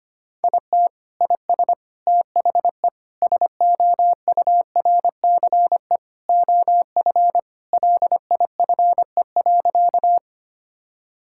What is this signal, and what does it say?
Morse code, heard as a keyed tone, about 25 wpm, 700 Hz. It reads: IT IS THE SOURCE OF LIFE.
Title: Sentences from Top 1000 Words - Review Part 5 - 25wpm